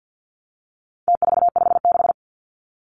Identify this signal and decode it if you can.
Morse code, heard as a keyed tone, about 50 wpm, 700 Hz. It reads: T456
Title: Numbers - Contest Style - 4-Digit 50wpm